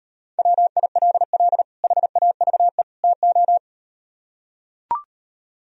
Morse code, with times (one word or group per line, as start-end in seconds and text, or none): 0.4-1.6 WILL
1.8-2.8 HAVE
3.0-3.6 TO